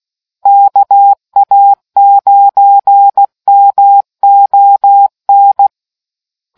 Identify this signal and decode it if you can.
Morse code, heard as a keyed tone, about 16 wpm, 800 Hz. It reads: KA9MON